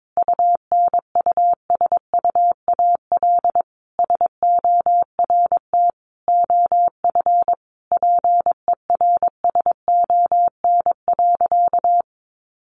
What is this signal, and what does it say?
Morse code, heard as a keyed tone, about 22 wpm, 700 Hz. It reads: UNUSUAL SORT OF PERSON.
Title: Sentences from Top 1400 Words 22wpm